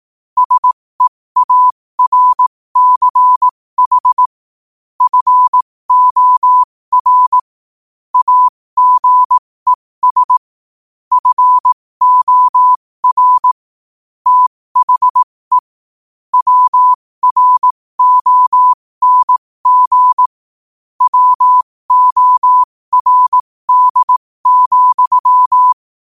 Morse code, heard as a keyed tone, about 18 wpm, spaced as for 15 wpm, 1000 Hz. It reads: SEARCH FOR AGES FOR THE WRONG WORD,
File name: qotdfiles / 15wpm